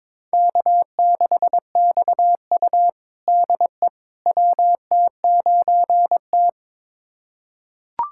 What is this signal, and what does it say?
Morse code, heard as a keyed tone, about 22 wpm, 700 Hz. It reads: K6XU DE WT9T
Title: USA Callsign DE Callsign - 4 Characters - 22wpm